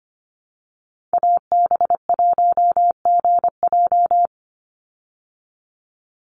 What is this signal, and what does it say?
Morse code, heard as a keyed tone, about 25 wpm, 700 Hz. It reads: AB1GJ
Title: Call Signs Repeated 3x - 25wpm